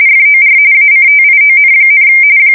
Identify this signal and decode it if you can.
RTTY, about 45.45 baud. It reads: FOR ACTION,
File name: fsk170 / USS-New-Jersey